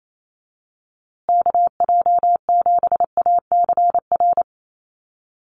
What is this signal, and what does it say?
Morse code, heard as a keyed tone, about 28 wpm, 700 Hz. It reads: KJ7ACR
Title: USA Call Signs Repeated 2x 28wpm